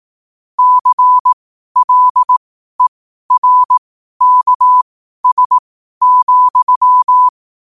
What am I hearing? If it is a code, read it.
Morse code, heard as a keyed tone, about 18 wpm, spaced as for 12 wpm, 1000 Hz. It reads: CLERKS,
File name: 12wpm